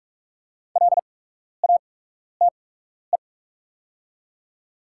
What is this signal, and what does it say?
Morse code, heard as a keyed tone, about 45 wpm, spaced as for 12 wpm, 690 Hz. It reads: LATE
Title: 8x Character Spacing - Top 300 Words - 45wpm